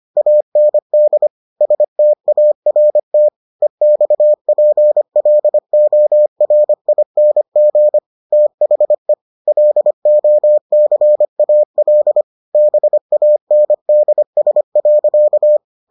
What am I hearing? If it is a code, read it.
Morse code, heard as a keyed tone, about 25 wpm, 600 Hz. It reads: AND START EXPLORING THE LOCAL BANDS.